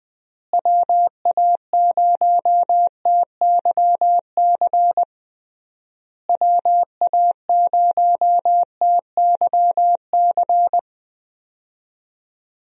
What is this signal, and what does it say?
Morse code, heard as a keyed tone, about 21 wpm, 700 Hz. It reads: WA0TYC WA0TYC
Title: USA Call Signs - 6 Characters - Repeated 3x - 20wpm